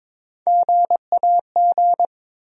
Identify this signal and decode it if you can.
Morse code, heard as a keyed tone, about 22 wpm, 700 Hz. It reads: GAG
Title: USA Call Sign Postfixes - 3 Letters 22wpm